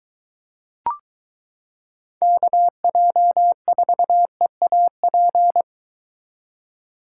Morse code, heard as a keyed tone, about 23 wpm, 700 Hz. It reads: KJ4EAP